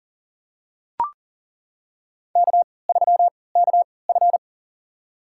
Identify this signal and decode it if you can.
Morse code, heard as a keyed tone, about 38 wpm, spaced as for 22 wpm, 700 Hz. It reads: K3KF
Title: USA Call Signs - 4 Characters - 3x Character Spacing - 40wpm